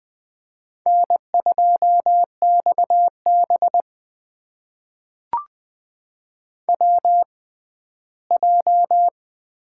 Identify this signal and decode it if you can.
Morse code, heard as a keyed tone, about 20 wpm, spaced as for 14 wpm, 700 Hz. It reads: N2XB WJ
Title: USA Call Signs - 4 Characters - 6x Character Spacing - 20wpm